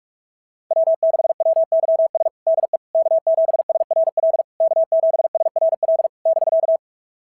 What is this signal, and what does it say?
Morse code, heard as a keyed tone, about 46 wpm, 690 Hz. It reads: W6WYS DE K7SRL K7SRL <BK>
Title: SOTA QSO Practice 45wpm